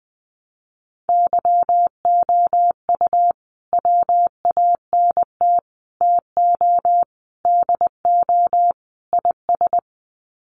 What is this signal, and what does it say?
Morse code, heard as a keyed tone, about 20 wpm, 700 Hz. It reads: YOU WANT TO DO IS
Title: Sets of 5 Words Encore2 20wpm